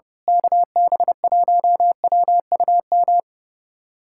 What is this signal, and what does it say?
Morse code, heard as a keyed tone, about 30 wpm, 700 Hz. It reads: KB1WUM
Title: USA Callsign DE Callsign 30wpm